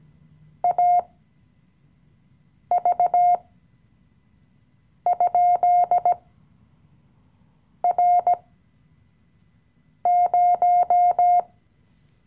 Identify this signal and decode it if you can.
Morse code, heard as a keyed tone, about 17 wpm, spaced as for 5 wpm, 710 Hz. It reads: AV?R0
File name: CW-Online-Kurs / Anfänger / Lektion 018